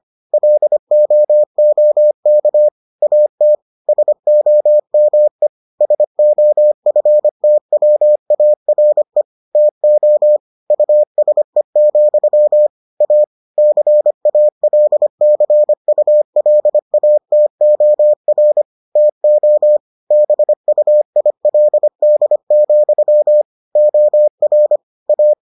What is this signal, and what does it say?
Morse code, heard as a keyed tone, about 25 wpm, 600 Hz. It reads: LOOK AT SOME SOFTWARE TO USE, A CALCULATOR TO BUILD, OR A